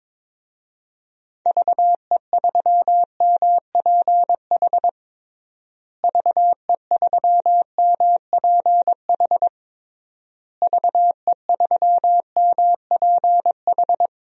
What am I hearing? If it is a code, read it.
Morse code, heard as a keyed tone, about 22 wpm, 700 Hz. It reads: VE3MPH VE3MPH VE3MPH
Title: DXCC-100 Call Signs - 6 Characters - Repeated 3x - 22wpm